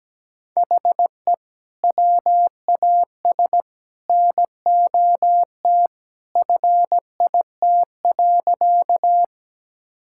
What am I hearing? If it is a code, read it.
Morse code, heard as a keyed tone, about 17 wpm, 700 Hz. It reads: HE WAS NOT FIT.
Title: Sentences from Top 1000 Words - Review Part 5 - 17wpm